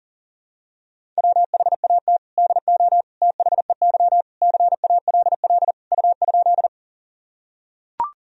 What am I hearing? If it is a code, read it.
Morse code, heard as a keyed tone, about 40 wpm, 700 Hz. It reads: WHAT DO THEY CALL U?
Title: Sentences from Top 100 Words and Abbreviations 40wpm